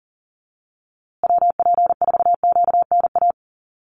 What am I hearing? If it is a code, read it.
Morse code, heard as a keyed tone, about 40 wpm, 700 Hz. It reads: WP4QNA
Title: USA Call Signs Repeated 3x 40wpm